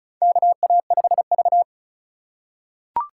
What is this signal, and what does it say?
Morse code, heard as a keyed tone, about 34 wpm, 700 Hz. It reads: KA5V
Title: USA Call Signs - 4 Characters - 2x Character Spacing - 35wpm